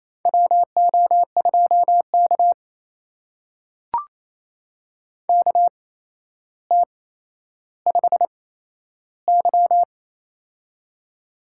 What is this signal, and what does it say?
Morse code, heard as a keyed tone, about 27 wpm, spaced as for 12 wpm, 700 Hz. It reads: WO2K KT5Y
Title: USA Call Signs - 8x Character Spacing - 28wpm